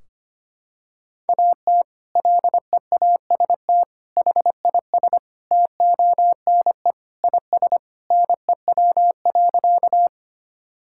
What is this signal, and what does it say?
Morse code, heard as a keyed tone, about 25 wpm, 700 Hz. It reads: AT LEAST HIS TONE IS NEW.
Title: Sentences from Top 1600 Words 25wpm